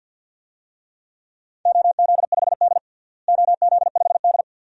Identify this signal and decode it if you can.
Morse code, heard as a keyed tone, about 48 wpm, 700 Hz. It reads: OZ5D OZ5D
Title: DXCC-100 Call Signs - 4 Characters - Repeated 2x - 50wpm